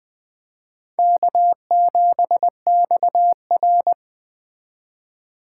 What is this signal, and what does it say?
Morse code, heard as a keyed tone, about 21 wpm, 700 Hz. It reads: K7XR